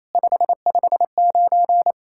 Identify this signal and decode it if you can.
Morse code, heard as a keyed tone, about 27 wpm, 700 Hz. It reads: 559
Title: SOTA QSO Practice 28wpm